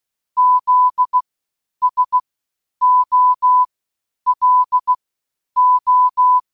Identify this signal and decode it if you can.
Morse code, heard as a keyed tone, about 16 wpm, spaced as for 9 wpm, 1000 Hz. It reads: ZSOLO